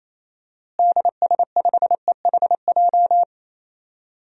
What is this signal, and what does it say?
Morse code, heard as a keyed tone, about 28 wpm, 700 Hz. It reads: DS5EHJ
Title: DXCC-100 Call Signs - 6 Characters - Repeated 2x - 28wpm